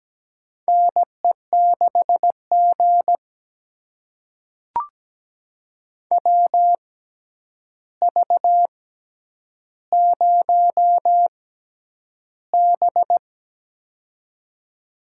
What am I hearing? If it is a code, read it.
Morse code, heard as a keyed tone, about 17 wpm, spaced as for 9 wpm, 700 Hz. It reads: NE6G WV0B
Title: USA Call Signs - 4 Characters - 6x Character Spacing - 17wpm